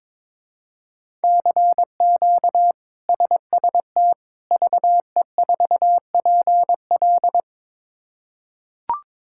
Morse code, heard as a keyed tone, about 22 wpm, 700 Hz. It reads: CQ SST VE4PL
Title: K1USN SST QSO Practice - 22wpm